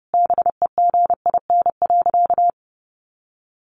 Morse code, heard as a keyed tone, about 30 wpm, 700 Hz. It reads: BEGIN.
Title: Sentences from Top 300 Words 30wpm